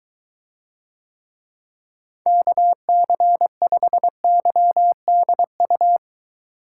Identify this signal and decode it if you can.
Morse code, heard as a keyed tone, about 23 wpm, 700 Hz. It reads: KC5YDU